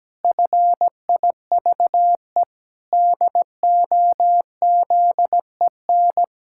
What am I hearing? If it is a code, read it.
Morse code, heard as a keyed tone, about 17 wpm, 700 Hz. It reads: FIVE DOZEN
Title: Pangram Challenge v2 - 17wpm